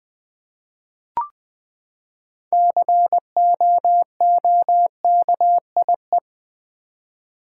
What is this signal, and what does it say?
Morse code, heard as a keyed tone, about 21 wpm, 700 Hz. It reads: COOKIE
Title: Six Letter Words - 20wpm